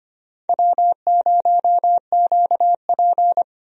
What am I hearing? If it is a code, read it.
Morse code, heard as a keyed tone, about 25 wpm, 700 Hz. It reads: W0QP
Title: USA Call Signs - 4 Characters - 25wpm